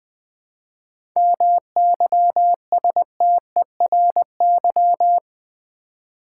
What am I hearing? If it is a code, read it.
Morse code, heard as a keyed tone, about 20 wpm, 700 Hz. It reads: MYSTERY
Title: 7 Letter Words - First 3 Letters Given - 20wpm